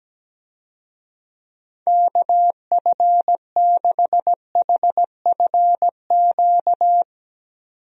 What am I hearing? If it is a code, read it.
Morse code, heard as a keyed tone, about 17 wpm, 700 Hz. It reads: KF6HFQ
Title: USA Call Signs - 6 Characters - 17wpm